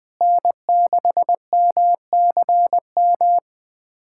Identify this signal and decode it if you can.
Morse code, heard as a keyed tone, about 20 wpm, 700 Hz. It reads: N6MCM
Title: USA Callsign DE Callsign - 5 Characters - 20wpm